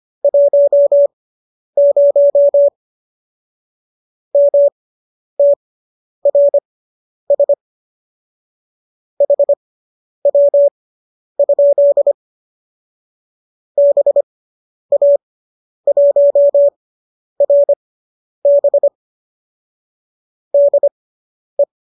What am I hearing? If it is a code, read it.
Morse code, heard as a keyed tone, about 25 wpm, spaced as for 10 wpm, 570 Hz. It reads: 10 MTRS HW? BA1RB DE